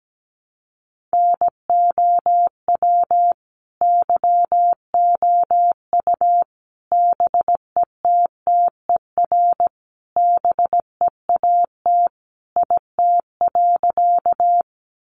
Morse code, heard as a keyed tone, about 17 wpm, 700 Hz. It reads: NOW YOU BETTER BEAT IT.
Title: Sentences from Top 1300 Words 17wpm